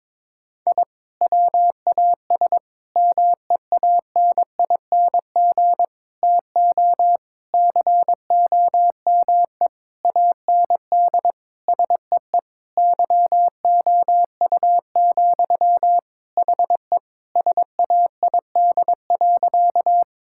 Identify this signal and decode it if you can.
Morse code, heard as a keyed tone, about 22 wpm, 700 Hz. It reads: I WAS MEANING TO COME AND SEE YOU, HE SAID.
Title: Sentences from Top 1000 Words - Review Part 6 - 22wpm